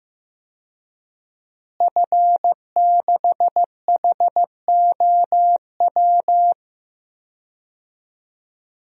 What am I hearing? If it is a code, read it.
Morse code, heard as a keyed tone, about 15 wpm, 700 Hz. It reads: F6HOW
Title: DXCC-100 Call Signs - 5 Characters - Repeated 2x - 15wpm